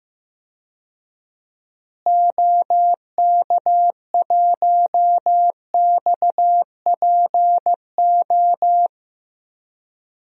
Morse code, heard as a keyed tone, about 15 wpm, 700 Hz. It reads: OK1XPO
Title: DXCC-100 Call Signs - 6 Characters - Repeated 2x - 15wpm